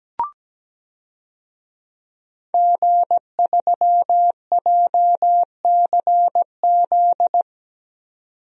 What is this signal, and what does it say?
Morse code, heard as a keyed tone, about 17 wpm, 700 Hz. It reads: G3JCZ
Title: DXCC-100 Call Signs - 5 Characters - 17wpm